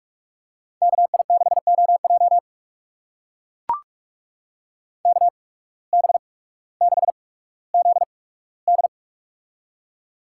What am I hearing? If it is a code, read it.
Morse code, heard as a keyed tone, about 44 wpm, spaced as for 19 wpm, 700 Hz. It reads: KI6OJ KB6ZD
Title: USA Call Signs - 5 Characters - 8x Character Spacing - 45wpm